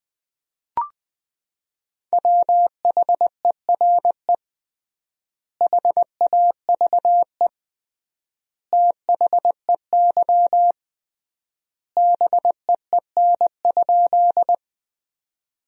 Morse code, heard as a keyed tone, about 20 wpm, 700 Hz. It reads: WHERE HAVE THEY BEEN?
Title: 3x Word-Spacing - Sentences from Top 200 Words - 20wpm